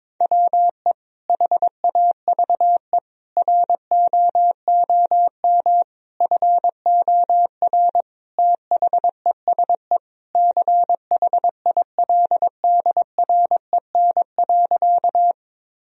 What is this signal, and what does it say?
Morse code, heard as a keyed tone, about 22 wpm, 700 Hz. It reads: WE HAVE ROOM FOR THESE CHILDREN.
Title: Sentences from Top 1000 Words - Review Part 1 - 22wpm